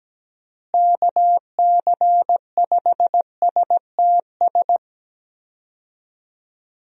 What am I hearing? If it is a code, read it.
Morse code, heard as a keyed tone, about 17 wpm, 700 Hz. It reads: KC5STS
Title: USA Call Signs - 6 Characters - Repeated 3x - 17wpm